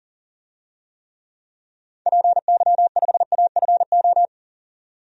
Morse code, heard as a keyed tone, about 40 wpm, 700 Hz. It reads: PY5AFO
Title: DXCC-100 Call Signs - 6 Characters - 40wpm